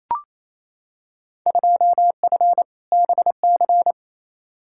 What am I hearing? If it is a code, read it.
Morse code, heard as a keyed tone, about 28 wpm, 700 Hz. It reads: 2F BC
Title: Field Day Exchanges Encore - 28wpm